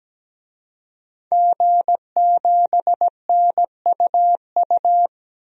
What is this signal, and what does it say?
Morse code, heard as a keyed tone, about 17 wpm, 700 Hz. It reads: G7NUU
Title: DXCC-100 Call Signs - 5 Characters - Repeated 2x - 17wpm